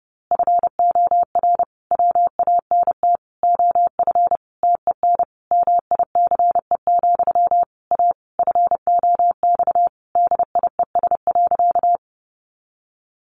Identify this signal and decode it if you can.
Morse code, heard as a keyed tone, about 30 wpm, 700 Hz. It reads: FOR WANT OF TEN MICE, A FOX DIES.
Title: A Sound Of Thunder 30wpm